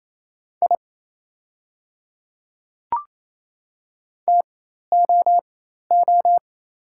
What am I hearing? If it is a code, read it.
Morse code, heard as a keyed tone, about 28 wpm, spaced as for 13 wpm, 700 Hz. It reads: I TOO
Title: 4x Character Spacing - Top 200 Words - 28wpm